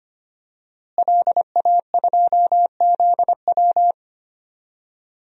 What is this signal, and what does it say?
Morse code, heard as a keyed tone, about 25 wpm, 700 Hz. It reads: LA2ZW
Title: DXCC-100 Call Signs - 5 Characters - Repeated 2x - 25wpm